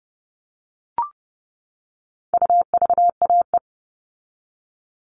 Morse code, heard as a keyed tone, about 30 wpm, 700 Hz. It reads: UVAE
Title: All Cut Numbers - 4-Digit 30wpm